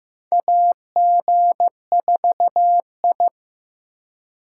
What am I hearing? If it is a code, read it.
Morse code, heard as a keyed tone, about 15 wpm, 700 Hz. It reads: AG4I
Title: USA Call Signs - 3x Character Spacing - 15wpm